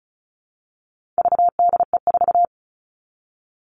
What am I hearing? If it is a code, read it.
Morse code, heard as a keyed tone, about 34 wpm, 700 Hz. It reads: VDE4
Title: All Cut Numbers - 4-Digit 35wpm